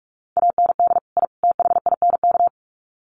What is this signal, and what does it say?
Morse code, heard as a keyed tone, about 44 wpm, 700 Hz. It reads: AND I THINK
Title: Sets of 3 Words 45wpm